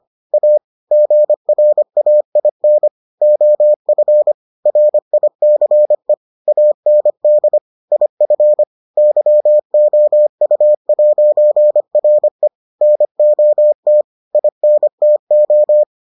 Morse code, heard as a keyed tone, about 25 wpm, 600 Hz. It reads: A GRAIN OF RICE AND IF YOU'RE NOT INTO